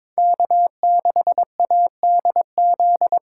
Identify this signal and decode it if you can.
Morse code, heard as a keyed tone, about 22 wpm, 700 Hz. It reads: K6ADZ
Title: USA Call Signs - 2x Character Spacing - 22wpm